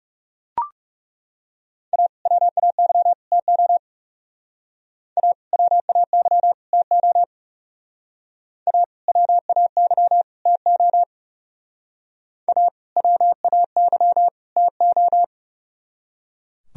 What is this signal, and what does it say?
Morse code, heard as a keyed tone, about 35 wpm, 700 Hz. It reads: A WAY TO A WAY TO A WAY TO A WAY TO
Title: Speed-Racing - Sets of 3 Words 30wpm